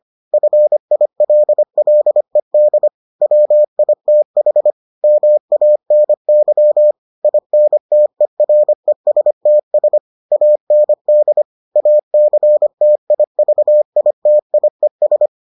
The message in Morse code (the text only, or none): FILLED WITH MANY INTERESTS AND ACTIVITIES